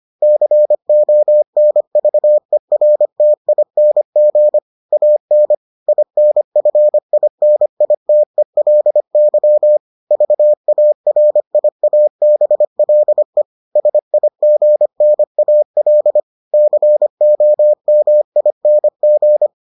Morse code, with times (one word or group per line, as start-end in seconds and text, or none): 0.2-4.6 CONVERTING
4.9-5.5 AN
5.9-9.8 INFINITELY
10.1-13.4 VARIABLE
13.8-16.2 SIGNAL
16.5-19.5 COMING